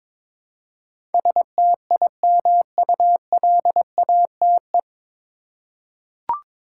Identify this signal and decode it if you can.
Morse code, heard as a keyed tone, about 22 wpm, 700 Hz. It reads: STIMULATE
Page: Morse Code Ninja practice 22wpm